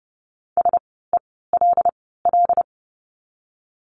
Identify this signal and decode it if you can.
Morse code, heard as a keyed tone, about 30 wpm, spaced as for 17 wpm, 700 Hz. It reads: SELL
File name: Lesson 17f - Letter u - Words 30wpm